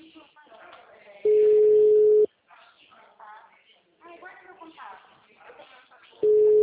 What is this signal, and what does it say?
Ringback tone (Europe, 425 Hz), with speech.
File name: out- -2009- - - .62394